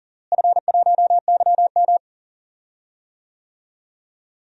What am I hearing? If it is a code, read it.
Morse code, heard as a keyed tone, about 40 wpm, 700 Hz. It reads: F1YM
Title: DXCC-100 Call Signs - 4 Characters - Repeated 2x - 40wpm